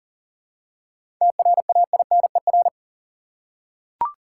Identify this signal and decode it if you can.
Morse code, heard as a keyed tone, about 40 wpm, 700 Hz. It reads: TRAINER
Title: Seven Letter Words - 40wpm